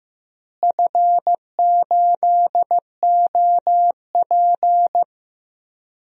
Morse code, heard as a keyed tone, about 14 wpm, 700 Hz. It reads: F8OP
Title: DXCC-100 Call Signs - 4 Characters - Repeated 3x - 15wpm